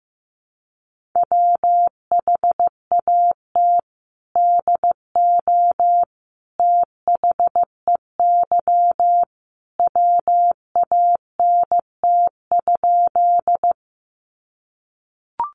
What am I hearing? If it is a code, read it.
Morse code, heard as a keyed tone, about 15 wpm, 700 Hz. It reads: WHAT DO THEY WANT?
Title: Sentences from Top 200 Words 15wpm